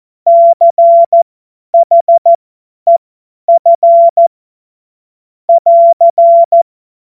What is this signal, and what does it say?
Morse code, heard as a keyed tone, about 14 wpm, spaced as for 10 wpm, 680 Hz. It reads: CHEF <AR>